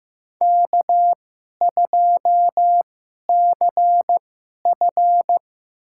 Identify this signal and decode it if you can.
Morse code, heard as a keyed tone, about 15 wpm, spaced as for 11 wpm, 700 Hz. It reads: K2CF
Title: USA Call Signs - 4 Characters - 2x Character Spacing - 15wpm